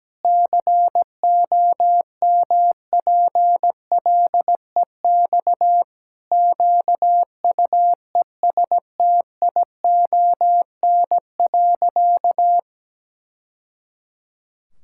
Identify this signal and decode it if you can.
Morse code, heard as a keyed tone, about 17 wpm, 700 Hz. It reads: COMPLEX QUESTION.
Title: Sentences from Top 1000 Words - Review Part 6 - 17wpm